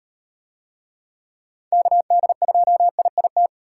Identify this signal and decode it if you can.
Morse code, heard as a keyed tone, about 38 wpm, 700 Hz. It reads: KD2IIT